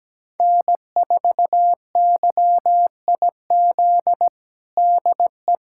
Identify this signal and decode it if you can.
Morse code, heard as a keyed tone, about 17 wpm, 700 Hz. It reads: N4YIZ DE
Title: USA Callsign DE Callsign - 5 Characters - 17wpm